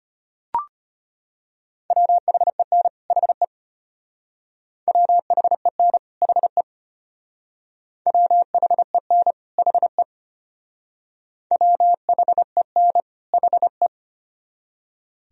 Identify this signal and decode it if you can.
Morse code, heard as a keyed tone, about 33 wpm, 700 Hz. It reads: WHEN HE WHEN HE WHEN HE WHEN HE